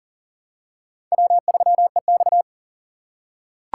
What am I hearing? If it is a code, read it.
Morse code, heard as a keyed tone, about 40 wpm, 700 Hz. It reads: W3EX
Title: USA Call Signs - 4x Character Spacing - 40wpm